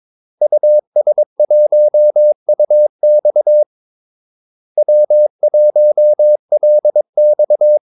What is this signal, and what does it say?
Morse code, heard as a keyed tone, about 22 wpm, 600 Hz. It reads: US1UX W1LX
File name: Callsigns-22wpm